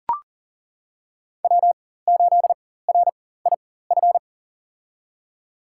Morse code, heard as a keyed tone, about 40 wpm, spaced as for 19 wpm, 700 Hz. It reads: W8RIF
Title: USA Call Signs - 5 Characters - 4x Character Spacing - 40wpm